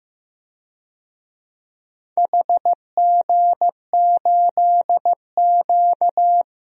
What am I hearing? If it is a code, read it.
Morse code, heard as a keyed tone, about 15 wpm, 700 Hz. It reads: HG8Q
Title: DXCC-100 Call Signs - 4 Characters - Repeated 2x - 15wpm